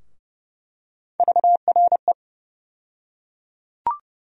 Morse code, heard as a keyed tone, about 30 wpm, 700 Hz. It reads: VRE
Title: USA Call Sign Postfixes - 3 Letters 30wpm